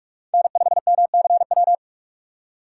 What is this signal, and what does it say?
Morse code, heard as a keyed tone, about 57 wpm, 700 Hz. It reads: N5MCW